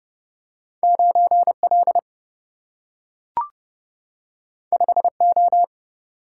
Morse code, heard as a keyed tone, about 30 wpm, 700 Hz. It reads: 9 L 5 O
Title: Single-Letter Single-Number 30wpm